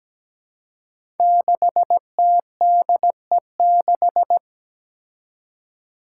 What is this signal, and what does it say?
Morse code, heard as a keyed tone, about 17 wpm, 700 Hz. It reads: 6TDE6
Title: Five Challenging Letter-Number Combinations 17wpm